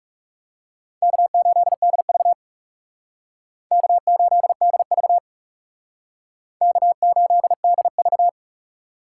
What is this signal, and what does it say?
Morse code, heard as a keyed tone, about 39 wpm, 700 Hz. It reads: K8DV K8DV K8DV